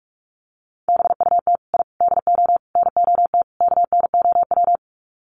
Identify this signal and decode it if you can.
Morse code, heard as a keyed tone, about 44 wpm, 700 Hz. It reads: BUT I DO NOT KNOW
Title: Sets of 5 Words 45wpm